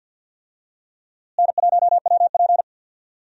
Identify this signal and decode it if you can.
Morse code, heard as a keyed tone, about 51 wpm, 700 Hz. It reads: N1WP
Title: USA Call Signs - 4 Characters - Repeated 3x - 50wpm